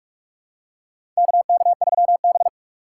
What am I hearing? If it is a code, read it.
Morse code, heard as a keyed tone, about 42 wpm, 700 Hz. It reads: KK3B
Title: USA Call Signs - 4 Characters - 6x Character Spacing - 45wpm